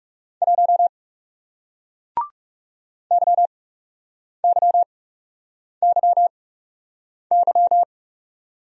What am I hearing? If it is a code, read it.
Morse code, heard as a keyed tone, about 40 wpm, spaced as for 8 wpm, 700 Hz. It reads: 1 YYYY